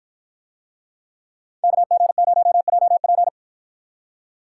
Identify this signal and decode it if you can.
Morse code, heard as a keyed tone, about 50 wpm, 700 Hz. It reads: KG0JP